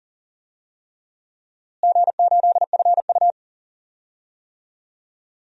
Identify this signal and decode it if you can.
Morse code, heard as a keyed tone, about 40 wpm, 700 Hz. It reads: G8FU
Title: DXCC-100 Call Signs - 4 Characters - Repeated 3x - 40wpm